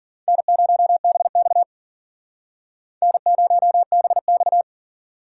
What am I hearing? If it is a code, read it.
Morse code, heard as a keyed tone, about 53 wpm, 700 Hz. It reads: N0BX N0BX